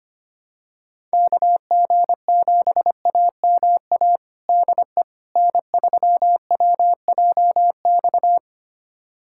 Morse code, heard as a keyed tone, about 25 wpm, 700 Hz. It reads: KG7AMA DE N3WJX
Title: USA Callsign DE Callsign 25wpm